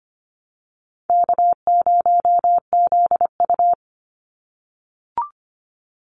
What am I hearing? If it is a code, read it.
Morse code, heard as a keyed tone, about 25 wpm, 700 Hz. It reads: K0ZU
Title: Easy Call Signs - 25wpm